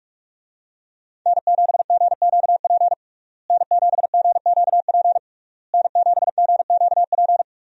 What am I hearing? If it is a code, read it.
Morse code, heard as a keyed tone, about 46 wpm, 700 Hz. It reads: N7GQP N7GQP N7GQP
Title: USA Call Signs - 5 Characters - Repeated 3x - 45wpm